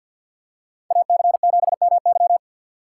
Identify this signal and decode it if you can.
Morse code, heard as a keyed tone, about 53 wpm, 700 Hz. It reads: AC7MY
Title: USA Call Signs - 5 Characters - 4x Character Spacing - 50wpm